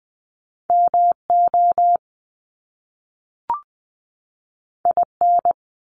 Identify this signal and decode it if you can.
Morse code, heard as a keyed tone, about 20 wpm, 700 Hz. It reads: MO IN
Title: US States 20wpm